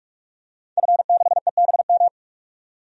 Morse code, heard as a keyed tone, about 46 wpm, 690 Hz. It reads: F6EBM
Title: DXCC-100 Call Signs - 5 Characters - 45wpm